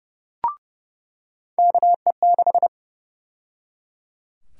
Morse code, heard as a keyed tone, about 30 wpm, 700 Hz. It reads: KE6
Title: USA Call Sign Prefixes - 30wpm